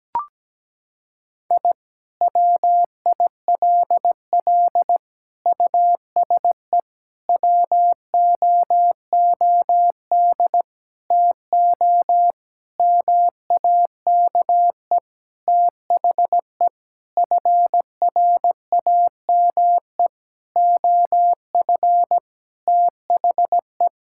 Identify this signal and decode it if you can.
Morse code, heard as a keyed tone, about 17 wpm, 700 Hz. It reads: I WILL USE WOOD TO MAKE THE FRAME OF THE
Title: Sentences from Top 1000 Words - Review Part 6 - 17wpm